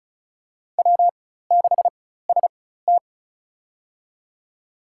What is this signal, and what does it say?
Morse code, heard as a keyed tone, about 34 wpm, spaced as for 16 wpm, 700 Hz. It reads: W6ST
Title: USA Call Signs - 4x Character Spacing - 35wpm